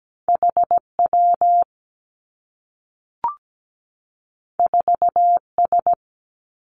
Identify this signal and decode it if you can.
Morse code, heard as a keyed tone, about 17 wpm, 700 Hz. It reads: H W 4 S
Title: Two Challenging Letter-Number Combinations 17wpm